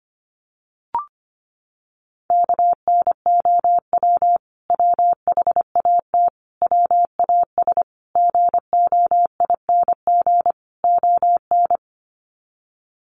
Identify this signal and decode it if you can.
Morse code, heard as a keyed tone, about 25 wpm, 700 Hz. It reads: KNOW WHAT WAS GOING ON
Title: Sets of 5 Words Encore2 25wpm